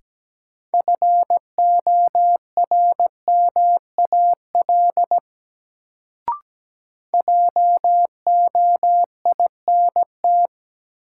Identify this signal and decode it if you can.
Morse code, heard as a keyed tone, about 17 wpm, 700 Hz. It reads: FORMAL JOINT
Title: Sentences from Top 1300 Words - New Word Drill Alt - 17wpm